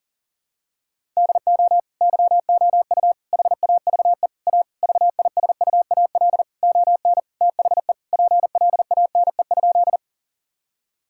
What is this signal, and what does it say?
Morse code, heard as a keyed tone, about 37 wpm, 700 Hz. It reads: DO YOU HAVE A VISUAL ON THE PLANE?
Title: Sentences from Top 1000 Words - Review Part 6 - 40wpm